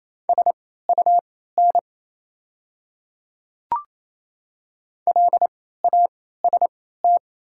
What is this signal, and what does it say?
Morse code, heard as a keyed tone, about 27 wpm, spaced as for 16 wpm, 700 Hz. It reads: SUN LAST
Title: 3x Character Spacing - Top 300 Words - 28wpm